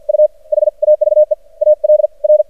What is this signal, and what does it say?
Morse code, heard as a keyed tone, about 53 wpm, spaced as for 42 wpm, 610 Hz. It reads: U HAVE ALR
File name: archived / CRSnet